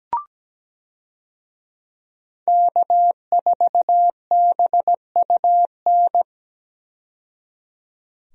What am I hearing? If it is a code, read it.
Morse code, heard as a keyed tone, about 17 wpm, 700 Hz. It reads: K4BUN
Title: USA Call Signs - 5 Characters - 17wpm